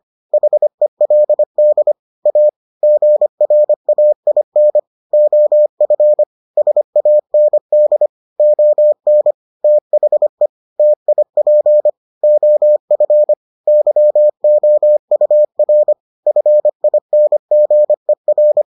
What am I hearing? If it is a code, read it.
Morse code, heard as a keyed tone, about 25 wpm, 600 Hz. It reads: HELD A GRAIN OF SAND ON THE TIP OF YOUR FINGER